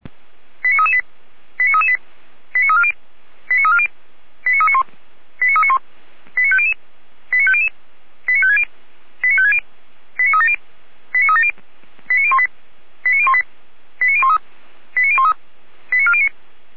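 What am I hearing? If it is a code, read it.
Five-tone selective call (ZVEI), heard as quick runs of five tones, tones of about 70 ms. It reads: 89338 89338 89340 89340 89391 89391 89500 89500 89560 89560 89370 89370 89018 89018 89013 89013 89409
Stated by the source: Received on an SDR; ZVEI five-tone call 89325.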